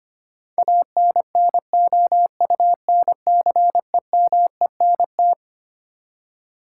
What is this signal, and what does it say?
Morse code, heard as a keyed tone, about 25 wpm, 700 Hz. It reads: ANNOUNCEMENT
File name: Twelve Letter Words - 25wpm